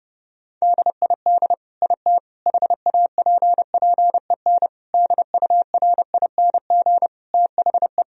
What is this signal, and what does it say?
Morse code, heard as a keyed tone, about 30 wpm, 700 Hz. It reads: DID IT HAPPEN DURING THE
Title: Sentences from Top 1000 Words - Review Part 2 - 30wpm